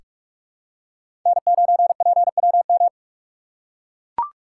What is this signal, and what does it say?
Morse code, heard as a keyed tone, about 43 wpm, 700 Hz. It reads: N9PWM
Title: USA Call Signs - 5 Characters - 4x Character Spacing - 45wpm